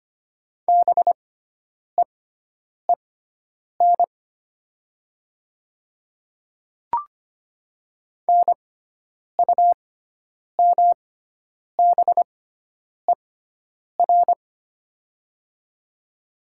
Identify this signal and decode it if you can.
Morse code, heard as a keyed tone, about 25 wpm, spaced as for 9 wpm, 700 Hz. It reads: BEEN NUMBER
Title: 6x Character Spacing - Top 100 Words - 25wpm